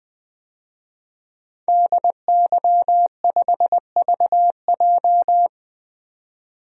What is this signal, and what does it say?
Morse code, heard as a keyed tone, about 20 wpm, 700 Hz. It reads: DY5VJ
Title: DXCC-100 Call Signs - 5 Characters - 20wpm